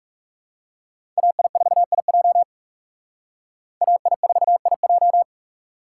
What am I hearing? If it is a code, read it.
Morse code, heard as a keyed tone, about 37 wpm, 700 Hz. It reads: AI4IJ AI4IJ
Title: Speed Racing - USA 2x2 Call Signs 30wpm